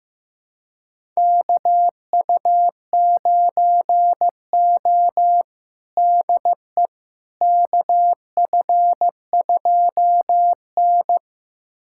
Morse code, heard as a keyed tone, about 15 wpm, 700 Hz. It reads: KU9O DE KF2N
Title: USA Callsign DE Callsign - 4 Characters - 15wpm